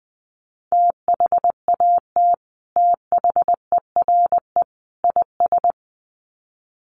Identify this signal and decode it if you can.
Morse code, heard as a keyed tone, about 20 wpm, 700 Hz. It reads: THAT THERE IS
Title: Sets of 3 Words 20wpm